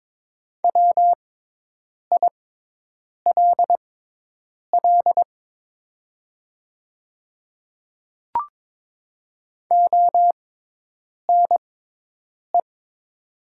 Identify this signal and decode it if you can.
Morse code, heard as a keyed tone, about 23 wpm, spaced as for 8 wpm, 700 Hz. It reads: WILL ONE
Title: 6x Character Spacing - Top 100 Words - 22wpm